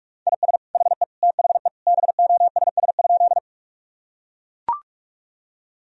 Morse code, heard as a keyed tone, about 41 wpm, 700 Hz. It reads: IS HE THE BOSS?
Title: Sentences from Top 1000 Words - Review Part 4 - 45wpm